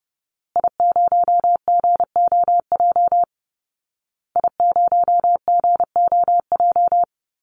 Morse code, heard as a keyed tone, about 30 wpm, 700 Hz. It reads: I0GOJ I0GOJ
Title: Call Signs Repeated 2x - 30wpm